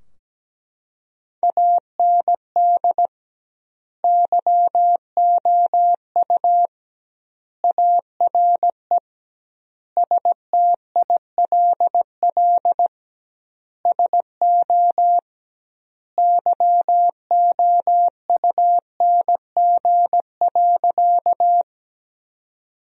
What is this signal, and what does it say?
Morse code, heard as a keyed tone, about 17 wpm, 700 Hz. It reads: AND YOU ARE STILL SO YOUNG.
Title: 2x Word-Spacing - Sentences from Top 400 Words - 17wpm